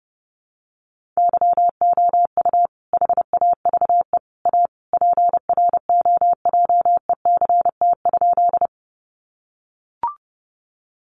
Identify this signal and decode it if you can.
Morse code, heard as a keyed tone, about 30 wpm, 700 Hz. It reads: YOU HAVE A PROJECT?
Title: Sentences from Top 1400 Words 30wpm